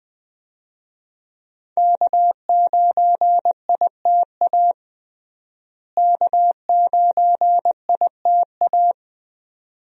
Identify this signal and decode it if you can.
Morse code, heard as a keyed tone, about 20 wpm, 700 Hz. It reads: K9ITA K9ITA
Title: USA Call Signs - 5 Characters - Repeated 2x - 20wpm